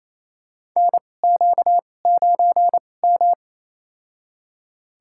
Morse code, heard as a keyed tone, about 28 wpm, spaced as for 20 wpm, 700 Hz. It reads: NQ9M
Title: USA Call Signs - 2x Character Spacing - 28wpm